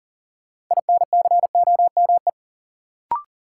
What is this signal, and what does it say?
Morse code, heard as a keyed tone, about 40 wpm, 700 Hz. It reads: INCOME
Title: Sentences from Top 1000 Words - Review Part 5 - New Word Drill - 40wpm